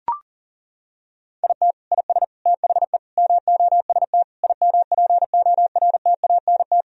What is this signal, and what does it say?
Morse code, heard as a keyed tone, about 40 wpm, 700 Hz. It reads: IT IS THE MOST IMPORTANT
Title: Sentences from Top 1000 Words - Review Part 1 - 40wpm